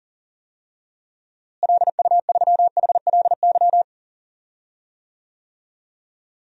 Morse code, heard as a keyed tone, about 38 wpm, 700 Hz. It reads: LU3HLY
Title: DXCC-100 Call Signs - 6 Characters - Repeated 3x - 40wpm